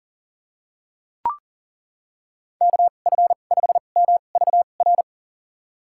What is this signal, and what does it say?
Morse code, heard as a keyed tone, about 38 wpm, spaced as for 28 wpm, 700 Hz. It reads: KF5MVR
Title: USA Call Signs - 6 Characters - 2x Character Spacing - 40wpm